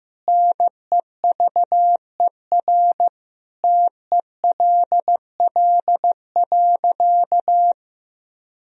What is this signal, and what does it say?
Morse code, heard as a keyed tone, about 15 wpm, 700 Hz. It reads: NEVER TELL.
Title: Sentences from Top 300 Words and Abbreviations 15wpm